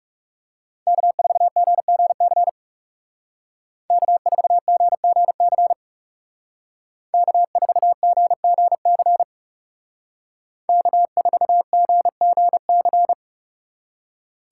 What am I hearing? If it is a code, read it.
Morse code, heard as a keyed tone, about 38 wpm, 700 Hz. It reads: K4GGC K4GGC K4GGC K4GGC
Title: Speed Racing - USA 1x3 Call Signs 30wpm